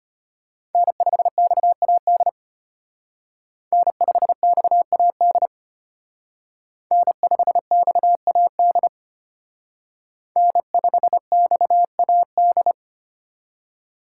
Morse code, heard as a keyed tone, about 29 wpm, 700 Hz. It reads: N5XAD N5XAD N5XAD N5XAD
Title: Speed Racing - USA 1x3 Call Signs 25wpm